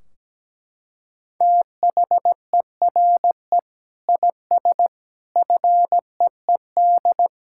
Morse code, heard as a keyed tone, about 17 wpm, 700 Hz. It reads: THERE IS FEED
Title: Sentences from Top 1600 Words 17wpm